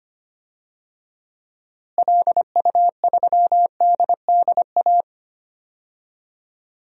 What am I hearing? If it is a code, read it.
Morse code, heard as a keyed tone, about 25 wpm, 700 Hz. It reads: LU3DDA
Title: DXCC-100 Call Signs - 6 Characters - Repeated 2x - 25wpm